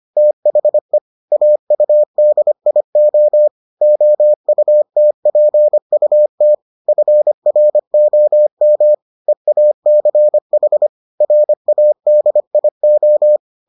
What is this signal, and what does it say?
Morse code, heard as a keyed tone, about 25 wpm, 600 Hz. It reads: THE AUDIO OUTPUT FROM EACH RADIO